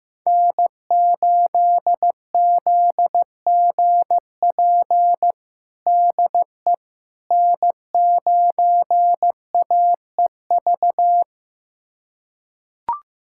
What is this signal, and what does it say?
Morse code, heard as a keyed tone, about 15 wpm, 700 Hz. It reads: N8ZGP DE N9AEV
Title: USA Callsign DE Callsign - 5 Characters - 15wpm